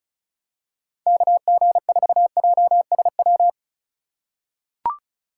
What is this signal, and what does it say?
Morse code, heard as a keyed tone, about 36 wpm, 690 Hz. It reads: KG4JSW
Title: USA Call Signs - 6 Characters - 6x Character Spacing - 35wpm